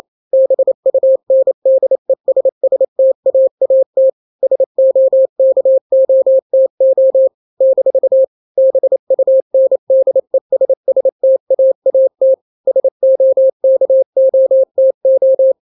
Morse code, heard as a keyed tone, about 28 wpm, 530 Hz. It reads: BUNDESSTAAT SOKOTO - BUNDESSTAAT SOKOTO
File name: daily / 27wpm